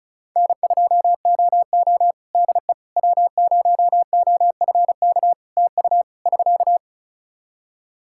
Morse code, heard as a keyed tone, about 36 wpm, 700 Hz. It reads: N2OO DE W0OFK TU <SK>
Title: K1USN SST QSO Practice 35wpm